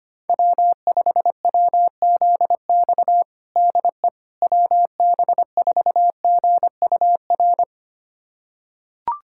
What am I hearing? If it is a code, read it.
Morse code, heard as a keyed tone, about 25 wpm, 700 Hz. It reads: W5WZX DE WB4GUR
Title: USA Callsign DE Callsign 25wpm